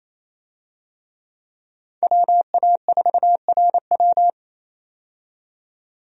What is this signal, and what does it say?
Morse code, heard as a keyed tone, about 28 wpm, 700 Hz. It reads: WA4RW
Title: USA Call Signs - 5 Characters - 28wpm